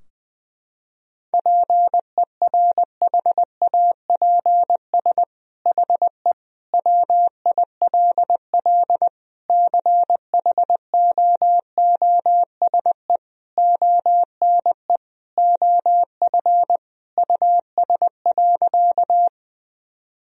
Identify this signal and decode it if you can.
Morse code, heard as a keyed tone, about 20 wpm, 700 Hz. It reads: PERHAPS HE WILL CHOOSE ONE OF US.
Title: Sentences from Top 1000 Words - Review Part 4 - 20wpm